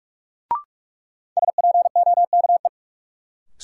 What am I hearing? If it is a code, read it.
Morse code, heard as a keyed tone, about 45 wpm, 700 Hz. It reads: SPOKE
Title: Sentences from Top 1200 Words - New Word Drill Alt - 45wpm